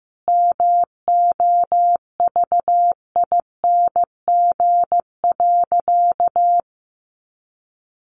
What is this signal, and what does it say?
Morse code, heard as a keyed tone, about 15 wpm, 700 Hz. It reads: MOVING.
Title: A Sound Of Thunder 15wpm